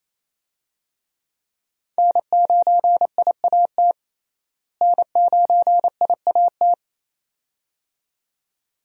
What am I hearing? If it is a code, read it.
Morse code, heard as a keyed tone, about 28 wpm, 700 Hz. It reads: N9IAT N9IAT
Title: USA Call Signs - 5 Characters - Repeated 2x - 28wpm